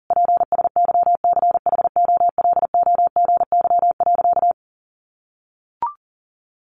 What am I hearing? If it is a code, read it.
Morse code, heard as a keyed tone, about 40 wpm, 700 Hz. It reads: PSYCHOLOGY.
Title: Sentences from Top 1400 Words 40wpm